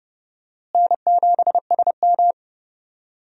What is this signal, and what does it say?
Morse code, heard as a keyed tone, about 31 wpm, 700 Hz. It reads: N7SM
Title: USA Call Signs - 4 Characters - Repeated 2x - 30wpm